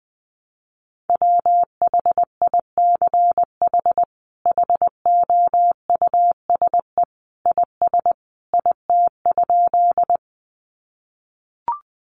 Morse code, heard as a keyed tone, about 20 wpm, 700 Hz. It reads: WHICH HOUSE IS IT?
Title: Sentences from Top 200 Words 20wpm